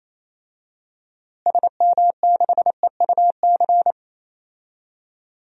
Morse code, heard as a keyed tone, about 27 wpm, 700 Hz. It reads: SM6EUC
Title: DXCC-100 Call Signs - 6 Characters - Repeated 2x - 28wpm